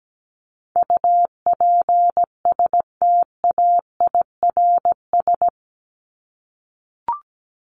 Morse code, heard as a keyed tone, about 17 wpm, 700 Hz. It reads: UPSTAIRS
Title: Top 1751-2000 Words 17wpm